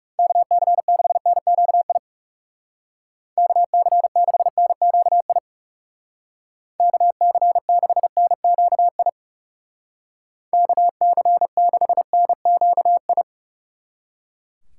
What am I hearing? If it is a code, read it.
Morse code, heard as a keyed tone, about 36 wpm, 700 Hz. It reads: KC6NQI KC6NQI KC6NQI KC6NQI
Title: Speed Racing - USA 2x3 Call Signs 30wpm